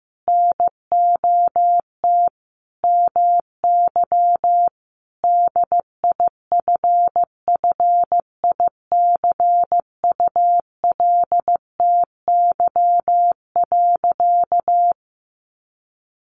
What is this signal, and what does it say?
Morse code, heard as a keyed tone, about 14 wpm, 700 Hz. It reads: NOT MY DIFFICULTY.
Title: Sentences from Top 1500 Words 15wpm